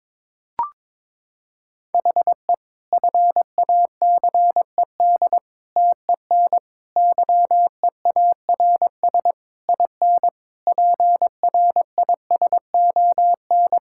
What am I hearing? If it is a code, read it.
Morse code, heard as a keyed tone, about 22 wpm, 700 Hz. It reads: HE FACED TEN YEARS IN PRISON
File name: Sentences with 9-Word Phrases - 22wpm